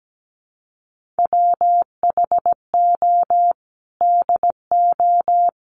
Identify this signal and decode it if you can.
Morse code, heard as a keyed tone, about 17 wpm, 700 Hz. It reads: WHO DO
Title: Sentences from Top 1500 Words 17wpm